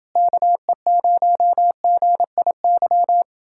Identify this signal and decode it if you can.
Morse code, heard as a keyed tone, about 29 wpm, 700 Hz. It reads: KE0GIY